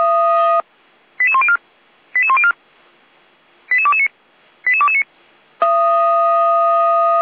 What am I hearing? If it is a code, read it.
Five-tone selective call (ZVEI), heard as quick runs of five tones, tones of about 70 ms. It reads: 88104 88104 88229 88229
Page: /455/SDR/ZVEI/